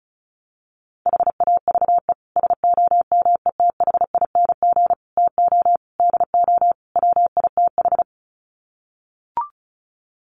Morse code, heard as a keyed tone, about 34 wpm, 700 Hz. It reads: HAVE SOMETHING TO DO WITH
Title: Sets of 5 Words Encore 35wpm